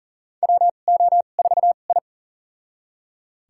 Morse code, heard as a keyed tone, about 40 wpm, spaced as for 29 wpm, 700 Hz. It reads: WO4I
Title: USA Call Signs - 4 Characters - 2x Character Spacing - 40wpm